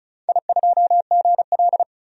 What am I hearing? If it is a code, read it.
Morse code, heard as a keyed tone, about 34 wpm, 700 Hz. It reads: I2GL
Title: DXCC-100 Call Signs - 4 Characters - 35wpm